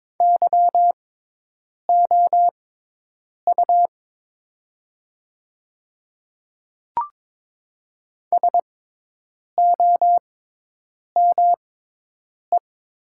Morse code, heard as a keyed tone, about 22 wpm, spaced as for 8 wpm, 700 Hz. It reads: YOU SOME
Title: 6x Character Spacing - Top 100 Words - 22wpm